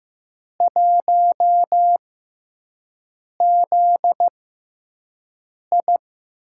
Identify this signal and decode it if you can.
Morse code, heard as a keyed tone, about 15 wpm, spaced as for 5 wpm, 700 Hz. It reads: 1ZI
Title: USA Call Signs - 6x Character Spacing - 15wpm